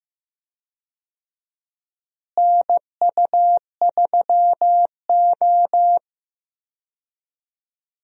Morse code, heard as a keyed tone, about 15 wpm, 700 Hz. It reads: NU3O
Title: USA Call Signs - 4 Characters - Repeated 3x - 15wpm